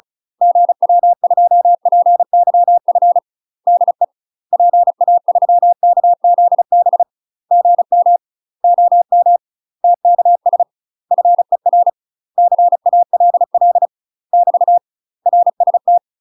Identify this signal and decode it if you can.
Morse code, heard as a keyed tone, about 36 wpm, 700 Hz. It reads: GW2PYF DE PA3KZB GM OM TKS FER CALL <BT> RST
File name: UK MASTER QSOs TRAINING / 35WPM / UK MASTER QSOs 0022